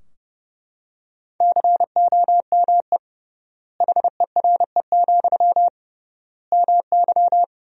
Morse code, heard as a keyed tone, about 29 wpm, 700 Hz. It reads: COME HERE, MY
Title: 3x Word-Spacing - Sentences from Top 300 Words - 30wpm